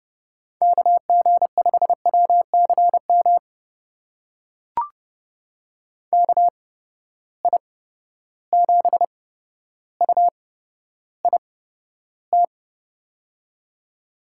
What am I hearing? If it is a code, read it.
Morse code, heard as a keyed tone, about 30 wpm, spaced as for 13 wpm, 700 Hz. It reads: KG5WCM KI7UIT
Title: USA Call Signs - 6 Characters - 8x Character Spacing - 30wpm